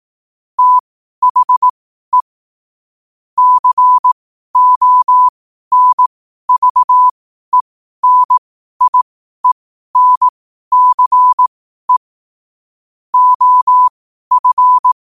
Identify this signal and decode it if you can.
Morse code, heard as a keyed tone, about 18 wpm, spaced as for 12 wpm, 1000 Hz. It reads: THE CONVENIENCE OF
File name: qotdfiles / 12wpm